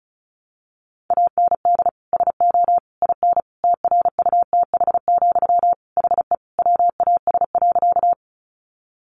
Morse code, heard as a keyed tone, about 34 wpm, 700 Hz. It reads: AND SO IN TRUTH, HE WAS.
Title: Sentences from Top 800 Words 35wpm